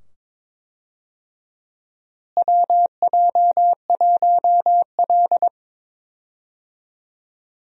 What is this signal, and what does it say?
Morse code, heard as a keyed tone, about 22 wpm, 700 Hz. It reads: WJ1L
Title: USA Call Signs - 4 Characters - Repeated 2x - 22wpm